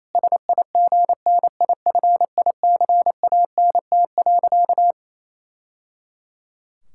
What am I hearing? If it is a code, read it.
Morse code, heard as a keyed tone, about 27 wpm, 700 Hz. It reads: SIGNIFICANT.
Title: Sentences from Top 1000 Words - Review Part 6 - 28wpm